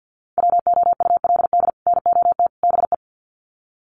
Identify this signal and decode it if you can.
Morse code, heard as a keyed tone, about 50 wpm, 700 Hz. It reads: WOULD NOT BE
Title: Sets of 3 Words 50wpm